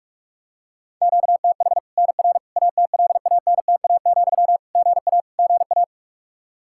Morse code, heard as a keyed tone, about 41 wpm, 700 Hz. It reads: QTH NR ATLANTA, GA GA
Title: K1USN SST QSO Practice 45wpm